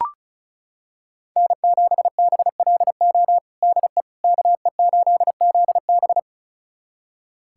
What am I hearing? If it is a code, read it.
Morse code, heard as a keyed tone, about 36 wpm, 700 Hz. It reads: N7BLO DE KE8ZB
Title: USA Callsign DE Callsign - 5 Characters - 35wpm